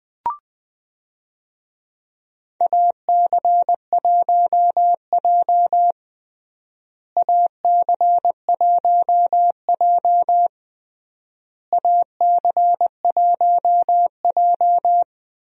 Morse code, heard as a keyed tone, about 20 wpm, 700 Hz. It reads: AC1J AC1J AC1J
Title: USA Call Signs - 4 Characters - Repeated 3x - 20wpm